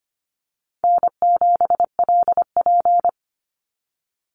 Morse code, heard as a keyed tone, about 25 wpm, 700 Hz. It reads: N7LP
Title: NA SOTA Activator Callsigns 25wpm